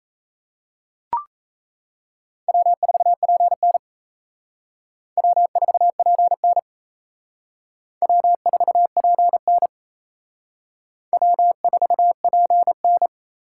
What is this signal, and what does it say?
Morse code, heard as a keyed tone, about 34 wpm, 700 Hz. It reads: W4PN W4PN W4PN W4PN
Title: Speed Racing - USA 1x2 Call Signs 28wpm